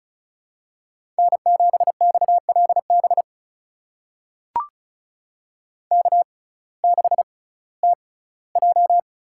Morse code, heard as a keyed tone, about 36 wpm, spaced as for 19 wpm, 700 Hz. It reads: N7XLB K6TJ
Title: USA Call Signs - 5 Characters - 6x Character Spacing - 35wpm